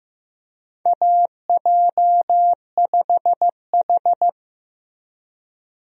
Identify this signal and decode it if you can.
Morse code, heard as a keyed tone, about 15 wpm, 700 Hz. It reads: AJ5H
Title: USA Call Signs - 4 Characters - Repeated 2x - 15wpm